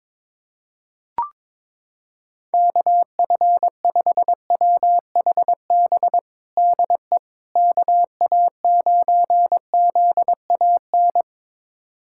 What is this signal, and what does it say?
Morse code, heard as a keyed tone, about 22 wpm, 700 Hz. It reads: KF5WHB DE KA9ZAN
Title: USA Callsign DE Callsign - 6 Characters - 22wpm